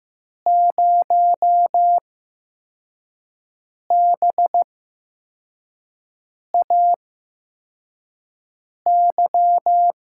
Morse code, heard as a keyed tone, about 15 wpm, spaced as for 4 wpm, 700 Hz. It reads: 0BAY
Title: USA Call Signs - 5 Characters - 8x Character Spacing - 15wpm